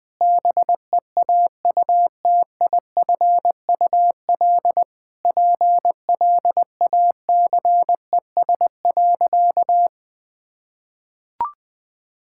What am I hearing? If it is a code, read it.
Morse code, heard as a keyed tone, about 21 wpm, 700 Hz. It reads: BEAUTIFUL PLACES.
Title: Sentences with 4-Word Phrases - 20wpm